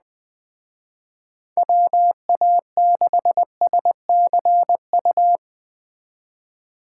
Morse code, heard as a keyed tone, about 20 wpm, 700 Hz. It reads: WA6SCU